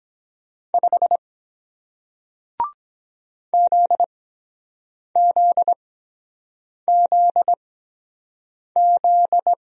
Morse code, heard as a keyed tone, about 23 wpm, spaced as for 7 wpm, 700 Hz. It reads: 5 ZZZZ